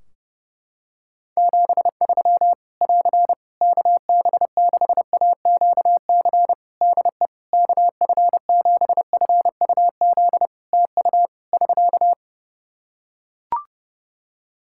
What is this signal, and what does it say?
Morse code, heard as a keyed tone, about 30 wpm, 700 Hz. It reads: 73 <AR> KB6AQC DE KF7FUZ TU <SK>
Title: Practice QSO - Exchange 4 - Ending - 30wpm